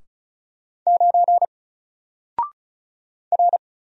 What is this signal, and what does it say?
Morse code, heard as a keyed tone, about 36 wpm, 700 Hz. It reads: 9 R
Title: Single Character (Letter:Number:Punctuation:Prosign) 35wpm